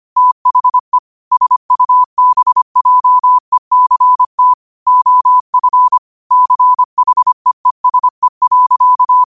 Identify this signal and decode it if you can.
Morse code, heard as a keyed tone, about 25 wpm, 1000 Hz. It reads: THE SUBJECT OF CHEESE.
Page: New quotes every day in morse code at 25 Words per minute.